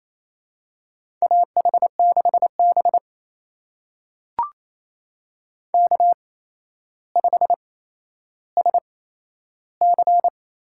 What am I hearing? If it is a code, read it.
Morse code, heard as a keyed tone, about 28 wpm, spaced as for 12 wpm, 700 Hz. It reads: AH6B K5SC
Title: USA Call Signs - 4 Characters - 8x Character Spacing - 28wpm